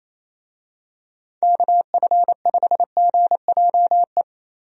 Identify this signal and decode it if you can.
Morse code, heard as a keyed tone, about 28 wpm, 700 Hz. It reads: KF5GJE
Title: USA Call Signs - 6 Characters - Repeated 3x - 28wpm